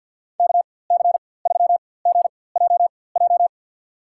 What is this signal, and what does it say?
Morse code, heard as a keyed tone, about 50 wpm, spaced as for 23 wpm, 700 Hz. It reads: KC3GJJ
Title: USA Call Signs - 4x Character Spacing - 50wpm